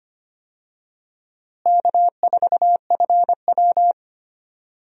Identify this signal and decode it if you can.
Morse code, heard as a keyed tone, about 25 wpm, 700 Hz. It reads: K4FW
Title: USA Call Signs - 4 Characters - Repeated 2x - 25wpm